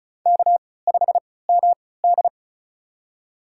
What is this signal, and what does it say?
Morse code, heard as a keyed tone, about 36 wpm, spaced as for 20 wpm, 700 Hz. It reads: K5MD
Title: USA Call Signs - 3x Character Spacing - 35wpm